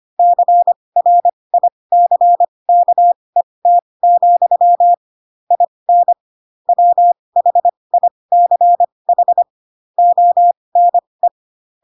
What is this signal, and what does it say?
Morse code, heard as a keyed tone, about 25 wpm, spaced as for 20 wpm, 700 Hz. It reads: CRICKET, IN WHICH ONE
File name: Pinocchio / Pinocchio ch4 15 25 ik1hge